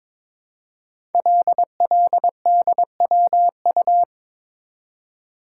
Morse code, heard as a keyed tone, about 22 wpm, 700 Hz. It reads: LLDWU
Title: Code Groups - 5 Letter:Numbers - 22wpm